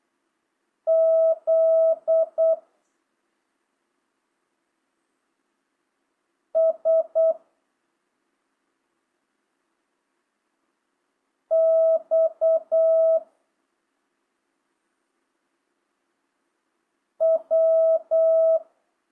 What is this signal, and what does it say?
Morse code, heard as a keyed tone, about 8 wpm, 650 Hz. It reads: Z S X W